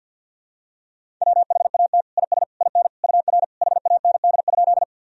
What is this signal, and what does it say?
Morse code, heard as a keyed tone, about 50 wpm, 700 Hz. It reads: WHAT IS IN UR HAND?
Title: Sentences from Top 200 Words and Abbreviations 50wpm